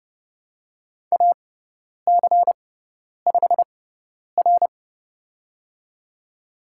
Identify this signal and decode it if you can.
Morse code, heard as a keyed tone, about 30 wpm, spaced as for 10 wpm, 700 Hz. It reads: AC5R
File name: Lesson 16g - Letter c - Call Signs 30wpm